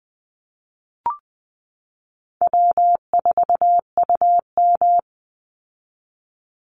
Morse code, heard as a keyed tone, about 20 wpm, 700 Hz. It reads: W4UM
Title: Easy Call Signs - 20wpm